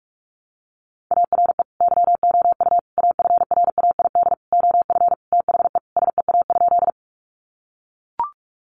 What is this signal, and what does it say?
Morse code, heard as a keyed tone, about 44 wpm, 700 Hz. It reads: ARE YOU AFRAID OF THE SEA?
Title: Sentences from Top 1500 Words 45wpm